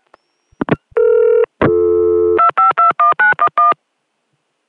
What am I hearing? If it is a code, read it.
DTMF, heard as telephone touch tones, tones of about 140 ms. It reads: 2521911